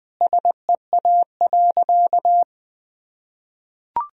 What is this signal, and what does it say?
Morse code, heard as a keyed tone, about 20 wpm, 700 Hz. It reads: SEA.
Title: Sentences from Top 1000 Words - Review Part 1 - 20wpm